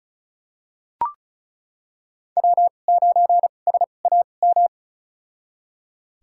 Morse code, heard as a keyed tone, about 36 wpm, spaced as for 25 wpm, 700 Hz. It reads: W9SAM
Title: USA Call Signs - 5 Characters - 2x Character Spacing - 35wpm